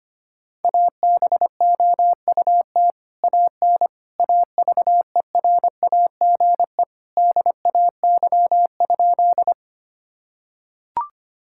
Morse code, heard as a keyed tone, about 25 wpm, 700 Hz. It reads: ABOUT AN AVERAGE DAY?
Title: Sentences from Top 1000 Words - Review Part 6 - 25wpm